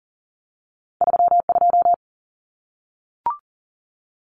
Morse code, heard as a keyed tone, about 40 wpm, 700 Hz. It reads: 32
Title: Common Cut Numbers - 2-Digit 40wpm